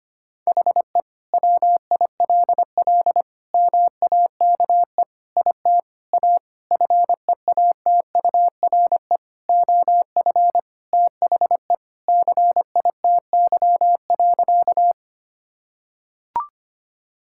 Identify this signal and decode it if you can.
Morse code, heard as a keyed tone, about 25 wpm, 700 Hz. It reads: HE WILL MAKE IT A FEATURE OF THE CITY.
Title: Sentences from Top 1000 Words - Review Part 5 - 25wpm